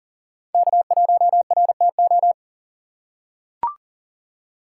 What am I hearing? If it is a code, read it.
Morse code, heard as a keyed tone, about 39 wpm, 700 Hz. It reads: K1RTO
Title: USA Call Signs - 5 Characters - 4x Character Spacing - 40wpm